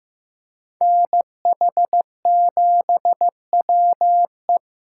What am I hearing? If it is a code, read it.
Morse code, heard as a keyed tone, about 18 wpm, 700 Hz. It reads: NH7WE
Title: Speed Racing - USA 2x2 Call Signs 15wpm